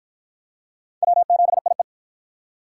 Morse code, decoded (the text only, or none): W7IE